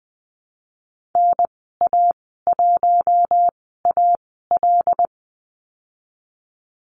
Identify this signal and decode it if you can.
Morse code, heard as a keyed tone, about 20 wpm, spaced as for 14 wpm, 700 Hz. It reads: NA1AL
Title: 082 Lesson 14g - Number 2 - Call Signs 20wpm